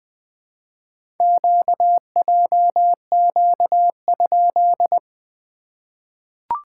Morse code, heard as a keyed tone, about 20 wpm, 700 Hz. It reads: QJQ?
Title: SOTA QSO Practice 20wpm